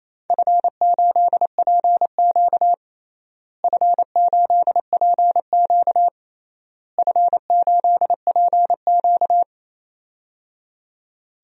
Morse code, heard as a keyed tone, about 28 wpm, 700 Hz. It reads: F8PQ F8PQ F8PQ
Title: DXCC-100 Call Signs - 4 Characters - Repeated 3x - 28wpm